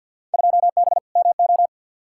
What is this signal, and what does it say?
Morse code, heard as a keyed tone, about 50 wpm, 700 Hz. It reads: 2B MO
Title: Field Day Exchanges - 50wpm